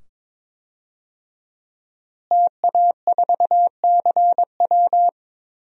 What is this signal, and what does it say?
Morse code, heard as a keyed tone, about 22 wpm, 700 Hz. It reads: TA4CW
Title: DXCC-100 Call Signs - 5 Characters - Repeated 2x - 22wpm